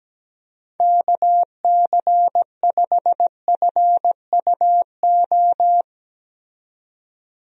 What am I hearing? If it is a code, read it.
Morse code, heard as a keyed tone, about 17 wpm, 700 Hz. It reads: KC5FUO
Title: USA Call Signs - 6 Characters - 17wpm